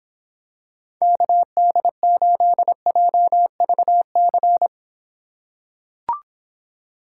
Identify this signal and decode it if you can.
Morse code, heard as a keyed tone, about 26 wpm, 700 Hz. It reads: KD8JVC